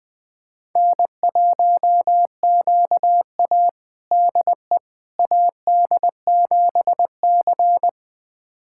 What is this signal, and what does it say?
Morse code, heard as a keyed tone, about 20 wpm, 700 Hz. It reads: N1QA DE AD7C
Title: USA Callsign DE Callsign - 4 Characters - 20wpm